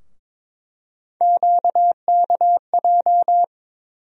Morse code, heard as a keyed tone, about 22 wpm, 700 Hz. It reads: QKJ
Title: USA Call Sign Postfixes - 3 Letters 22wpm